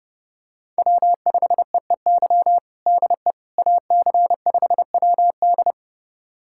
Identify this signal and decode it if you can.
Morse code, heard as a keyed tone, about 30 wpm, 700 Hz. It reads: W5EEY DE AC5WD
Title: USA Callsign DE Callsign - 5 Characters - 30wpm